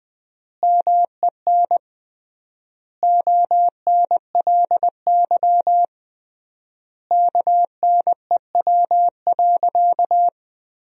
Morse code, heard as a keyed tone, about 19 wpm, 700 Hz. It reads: MEN ONLY KNEW.
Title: 3x Word-Spacing - Sentences from Top 500 Words - 20wpm